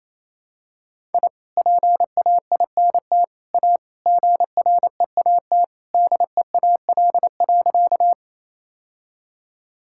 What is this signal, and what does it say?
Morse code, heard as a keyed tone, about 28 wpm, 700 Hz. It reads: I PAINT A GREAT DEAL.
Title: Sentences from Top 1000 Words - Review Part 6 - 28wpm